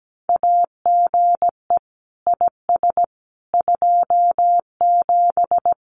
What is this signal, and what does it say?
Morse code, heard as a keyed tone, about 17 wpm, 700 Hz. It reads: AGE IS 27
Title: Common QSO Phrases Encore 17wpm